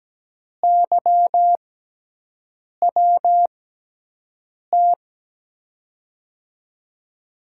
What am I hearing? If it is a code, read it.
Morse code, heard as a keyed tone, about 17 wpm, spaced as for 6 wpm, 700 Hz. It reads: YWT
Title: USA Call Signs - 6 Characters - 6x Character Spacing - 17wpm